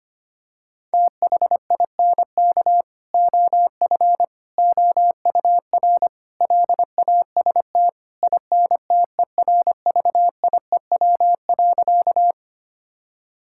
Morse code, heard as a keyed tone, about 25 wpm, 700 Hz. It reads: THINK OF OUR LAST INTERVIEW.
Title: Sentences from Top 1600 Words 25wpm